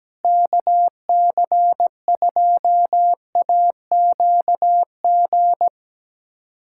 Morse code, heard as a keyed tone, about 17 wpm, 700 Hz. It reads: KC2AQG
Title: USA Callsign DE Callsign - 6 Characters - 17wpm